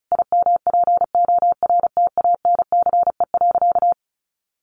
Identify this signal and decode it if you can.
Morse code, heard as a keyed tone, about 34 wpm, 700 Hz. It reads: IMPORTANCE.
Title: Sentences from Top 1200 Words 35wpm